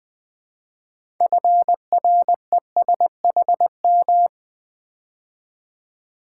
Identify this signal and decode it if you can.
Morse code, heard as a keyed tone, about 20 wpm, 700 Hz. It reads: FRESHM
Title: Build A Word - 3 Letters At A Time - 8-Letter Words - 20wpm